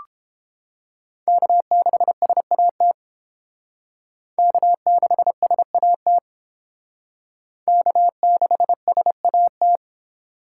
Morse code, heard as a keyed tone, about 29 wpm, 700 Hz. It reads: K6SAT K6SAT K6SAT